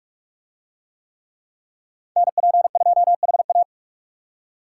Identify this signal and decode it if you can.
Morse code, heard as a keyed tone, about 41 wpm, 700 Hz. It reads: NP2HA